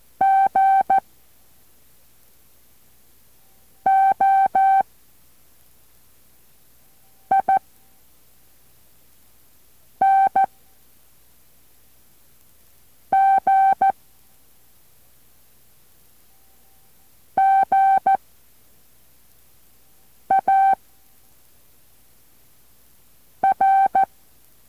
Morse code, heard as a keyed tone, about 14 wpm, 780 Hz. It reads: G O I N G G A R
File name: intl morse code2